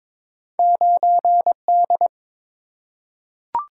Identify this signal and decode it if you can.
Morse code, heard as a keyed tone, about 22 wpm, 700 Hz. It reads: 9D
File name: practice / 22wpm / Two Letter-Number Combinations - 22wpm